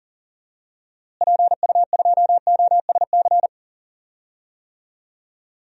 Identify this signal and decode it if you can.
Morse code, heard as a keyed tone, about 38 wpm, 700 Hz. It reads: PU2OSC
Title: DXCC-100 Call Signs - 6 Characters - Repeated 2x - 40wpm